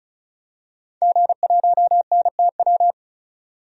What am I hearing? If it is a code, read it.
Morse code, heard as a keyed tone, about 36 wpm, 700 Hz. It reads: G1NTW
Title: DXCC-100 Call Signs - 5 Characters - 35wpm